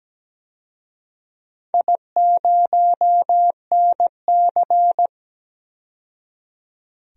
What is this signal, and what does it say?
Morse code, heard as a keyed tone, about 17 wpm, 700 Hz. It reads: I0NC
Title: DXCC-100 Call Signs - 4 Characters - 17wpm